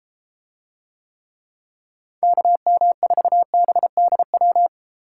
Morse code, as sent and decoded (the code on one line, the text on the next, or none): -.- -- ....- -... -.. .--
KM4BDW